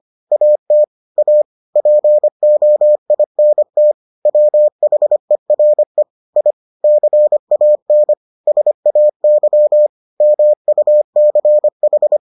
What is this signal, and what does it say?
Morse code, heard as a keyed tone, about 25 wpm, 600 Hz. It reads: AT A POINT WHERE I CAN SAY MUCH